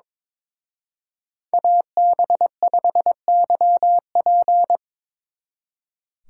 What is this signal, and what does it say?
Morse code, heard as a keyed tone, about 26 wpm, 700 Hz. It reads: AB5YP
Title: Speed Racing - USA 2x2 Call Signs 22wpm